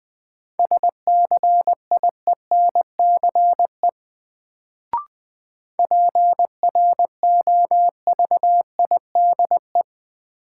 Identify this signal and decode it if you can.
Morse code, heard as a keyed tone, about 20 wpm, 700 Hz. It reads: SCIENCE PROVIDE
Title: Sentences from Top 1000 Words - Review Part 2 - New Word Drill - 20wpm